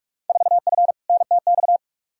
Morse code, heard as a keyed tone, about 46 wpm, 700 Hz. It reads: 4F NTX
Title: Field Day Exchanges Encore - 45wpm